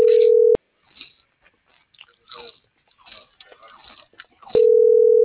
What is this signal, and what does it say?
Ringback tone (North America), 440 + 480 Hz.